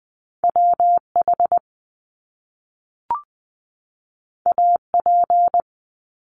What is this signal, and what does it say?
Morse code, heard as a keyed tone, about 20 wpm, 700 Hz. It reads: WH AP
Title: Top Two Letter Combinations in Words 20wpm